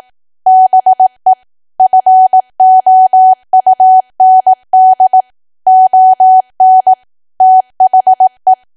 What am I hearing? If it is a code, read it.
Morse code, heard as a keyed tone, about 18 wpm, 750 Hz. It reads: BE FOUND ON THE
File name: cwa / W1AW-Level2 / 18WPM